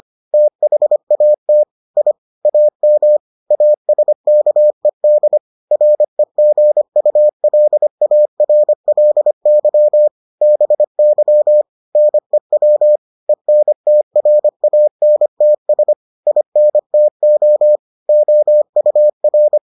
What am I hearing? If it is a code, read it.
Morse code, heard as a keyed tone, about 25 wpm, 600 Hz. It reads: THAT I AM ASKED REGULARLY BY NEW ENTRANTS INTO OUR